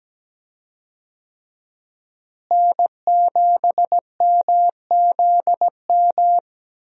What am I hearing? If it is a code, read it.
Morse code, heard as a keyed tone, about 17 wpm, 700 Hz. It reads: N7MZM
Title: USA Call Signs - 5 Characters - Repeated 3x - 17wpm